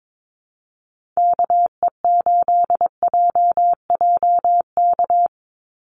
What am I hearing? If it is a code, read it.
Morse code, heard as a keyed tone, about 22 wpm, 700 Hz. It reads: KE8JJK
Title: USA Call Signs 22wpm